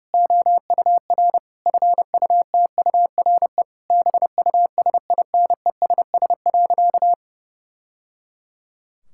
Morse code, heard as a keyed tone, about 30 wpm, 700 Hz. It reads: OUR FUTURE BUSINESS.
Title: Sentences from Top 1000 Words - Review Part 6 - 30wpm